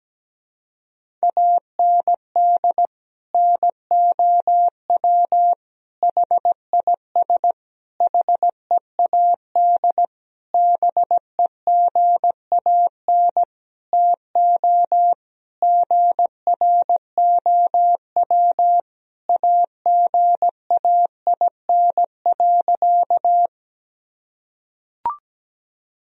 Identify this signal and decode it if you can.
Morse code, heard as a keyed tone, about 17 wpm, 700 Hz. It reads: AND NOW HIS HEAD BEGAN TO GROW AGAIN.
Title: Sentences from Top 1000 Words - Review Part 1 - 17wpm